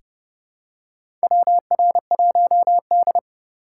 Morse code, heard as a keyed tone, about 29 wpm, 700 Hz. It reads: WR1D
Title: USA Call Signs - 4 Characters - 6x Character Spacing - 30wpm